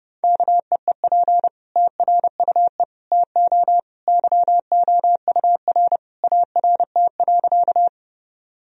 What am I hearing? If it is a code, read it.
Morse code, heard as a keyed tone, about 30 wpm, 700 Hz. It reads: KEEP TRUE TO YOUR ART.
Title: Sentences from Top 1000 Words - Review Part 3 - 30wpm